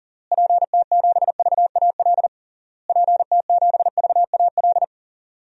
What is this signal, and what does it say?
Morse code, heard as a keyed tone, about 40 wpm, 700 Hz. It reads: PT7VAL PT7VAL
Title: DXCC-100 Call Signs - 6 Characters - Repeated 3x - 40wpm